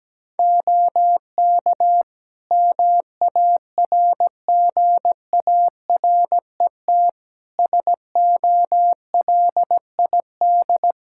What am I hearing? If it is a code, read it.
Morse code, heard as a keyed tone, about 17 wpm, 700 Hz. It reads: OK MARGARET SOLID
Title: Practice QSO - Exchange 3 - More Info - 17wpm